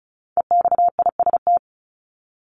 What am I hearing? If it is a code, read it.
Morse code, heard as a keyed tone, about 34 wpm, 700 Hz. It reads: EXIST
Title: ING Suffix - Top 100 Words with Base Words 35wpm